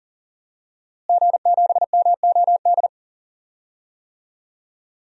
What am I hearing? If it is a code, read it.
Morse code, heard as a keyed tone, about 40 wpm, 700 Hz. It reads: G7MOD
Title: DXCC-100 Call Signs - 5 Characters - Repeated 3x - 40wpm